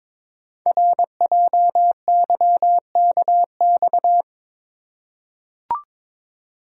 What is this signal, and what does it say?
Morse code, heard as a keyed tone, about 22 wpm, 700 Hz. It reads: RJYKX
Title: Code Groups - 5 Letters - 22wpm